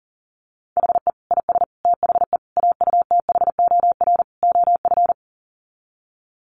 Morse code, heard as a keyed tone, about 40 wpm, 700 Hz. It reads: HE IS THE AUTHOR OF
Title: Sets of 5 Words Encore2 40wpm